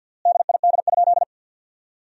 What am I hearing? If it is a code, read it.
Morse code, heard as a keyed tone, about 50 wpm, 700 Hz. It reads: DID?
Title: 3x Word-Spacing - Sentences from Top 100 Words - 50wpm